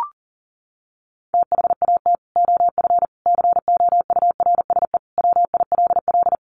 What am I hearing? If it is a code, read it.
Morse code, heard as a keyed tone, about 40 wpm, 700 Hz. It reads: THAT OF COURSE WILL
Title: Sentences from Top 1500 Words 40wpm